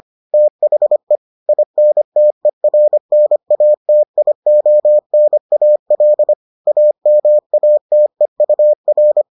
Morse code, heard as a keyed tone, about 25 wpm, 600 Hz. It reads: THE INTERNATIONAL AMATEUR